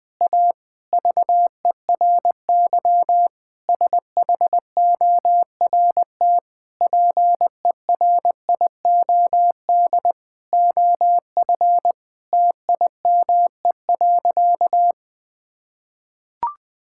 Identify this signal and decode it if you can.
Morse code, heard as a keyed tone, about 20 wpm, 700 Hz. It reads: A VERY SHORT PERIOD OF TIME.
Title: Sentences with 7-Word Phrases - 20wpm